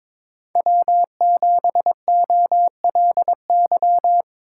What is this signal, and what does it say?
Morse code, heard as a keyed tone, about 22 wpm, 700 Hz. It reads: W7OLY
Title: USA Call Signs - 4x Character Spacing - 22wpm